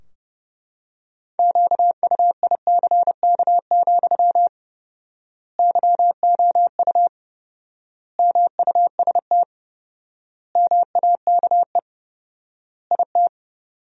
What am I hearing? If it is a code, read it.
Morse code, heard as a keyed tone, about 30 wpm, 700 Hz. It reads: QUICK, YOU MUST MAKE IT
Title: 4x Word-Spacing - Sentences from Top 500 Words - 30wpm